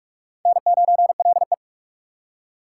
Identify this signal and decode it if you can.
Morse code, heard as a keyed tone, about 41 wpm, 700 Hz. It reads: N9LE